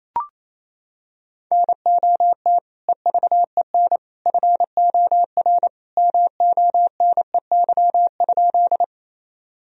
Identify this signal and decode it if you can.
Morse code, heard as a keyed tone, about 28 wpm, 700 Hz. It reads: NOT EVEN FOR MONEY?
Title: Sentences from Top 1000 Words - Review Part 1 - 28wpm